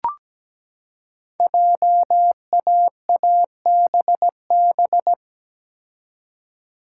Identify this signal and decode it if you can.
Morse code, heard as a keyed tone, about 17 wpm, 700 Hz. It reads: JAABB
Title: Code Groups - 5 Letters - 17wpm